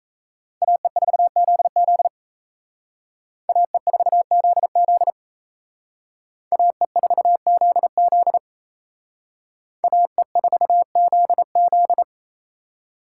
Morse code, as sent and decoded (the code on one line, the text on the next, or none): .- . ....- --.. --.. / .- . ....- --.. --.. / .- . ....- --.. --.. / .- . ....- --.. --..
AE4ZZ AE4ZZ AE4ZZ AE4ZZ